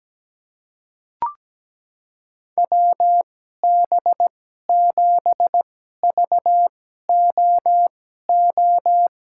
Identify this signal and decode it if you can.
Morse code, heard as a keyed tone, about 17 wpm, spaced as for 12 wpm, 700 Hz. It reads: WB7VOO
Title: USA Call Signs - 2x Character Spacing - 17wpm